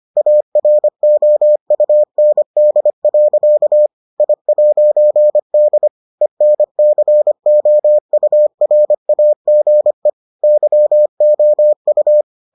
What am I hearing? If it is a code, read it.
Morse code, heard as a keyed tone, about 25 wpm, 600 Hz. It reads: AROUND. I'D ENCOURAGE YOU